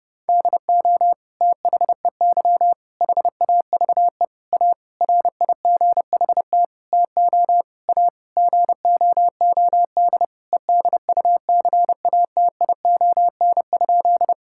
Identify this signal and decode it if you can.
Morse code, heard as a keyed tone, about 30 wpm, 700 Hz. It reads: DO THEY HAVE A RIGHT TO A GOOD EDUCATION?
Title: Sentences from Top 1000 Words - Review Part 6 - 30wpm